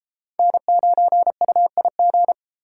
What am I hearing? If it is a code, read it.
Morse code, heard as a keyed tone, about 32 wpm, 700 Hz. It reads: N9UIG